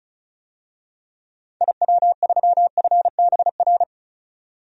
Morse code, heard as a keyed tone, about 34 wpm, 700 Hz. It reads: IW3FBR
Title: DXCC-100 Call Signs - 6 Characters - 35wpm